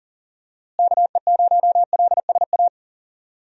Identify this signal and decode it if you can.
Morse code, heard as a keyed tone, about 40 wpm, 700 Hz. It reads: KE0LSA
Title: USA Call Signs - 6 Characters - Repeated 2x - 40wpm